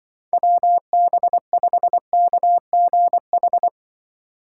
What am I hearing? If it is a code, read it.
Morse code, heard as a keyed tone, about 23 wpm, 700 Hz. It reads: WB5KGH
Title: Speed Racing - USA 2x3 Call Signs 20wpm